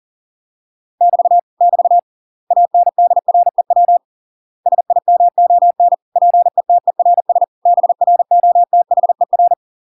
Morse code, heard as a keyed tone, about 40 wpm, 700 Hz. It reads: <BT> <BT> ANDREW SIMON PETERS BROTHER